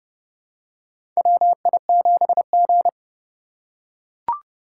Morse code, heard as a keyed tone, about 30 wpm, 700 Hz. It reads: WI7G
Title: USA Call Signs - 6x Character Spacing - 30wpm